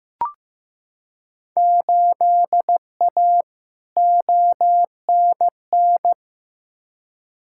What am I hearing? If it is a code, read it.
Morse code, heard as a keyed tone, about 15 wpm, 700 Hz. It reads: 8A ONN
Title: Field Day Exchanges Encore - 15wpm